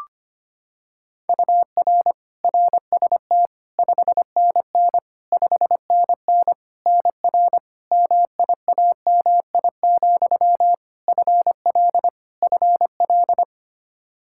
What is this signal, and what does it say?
Morse code, heard as a keyed tone, about 25 wpm, 700 Hz. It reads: UR RST 5NN 5NN NR MIAMI, FL FL
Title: K1USN SST QSO Practice 25wpm